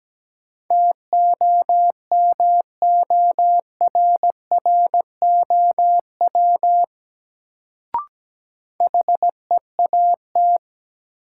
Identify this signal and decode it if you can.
Morse code, heard as a keyed tone, about 17 wpm, 700 Hz. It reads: TOMORROW HEAT
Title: Sentences from Top 1100 Words - New Word Drill Alt - 17wpm